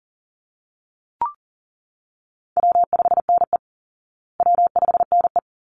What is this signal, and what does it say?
Morse code, heard as a keyed tone, about 40 wpm, 700 Hz. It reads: W5NE W5NE
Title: Easy Call Signs Repeated 2x - 40wpm